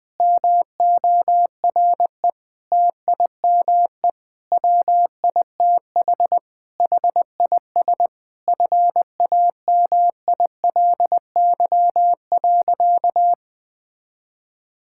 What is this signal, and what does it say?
Morse code, heard as a keyed tone, about 21 wpm, 700 Hz. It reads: MORE TIME WITH HIS FAMILY.
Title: Sentences with 7-Word Phrases - 20wpm